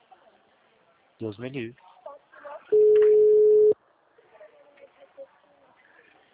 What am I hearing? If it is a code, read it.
Ringback tone (Europe, 425 Hz), with speech.